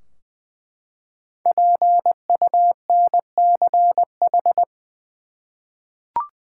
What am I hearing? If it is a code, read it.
Morse code, heard as a keyed tone, about 20 wpm, 700 Hz. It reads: PUNCH
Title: Five Letter Words - 20wpm